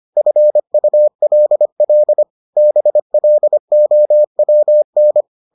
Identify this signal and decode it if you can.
Morse code, heard as a keyed tone, about 25 wpm, 600 Hz. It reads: FULL BLOWN